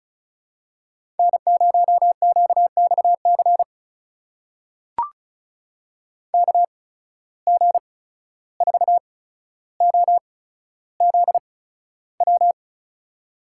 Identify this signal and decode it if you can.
Morse code, heard as a keyed tone, about 36 wpm, spaced as for 14 wpm, 700 Hz. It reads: N0QXC KG4OZW
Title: USA Call Signs - 8x Character Spacing - 35wpm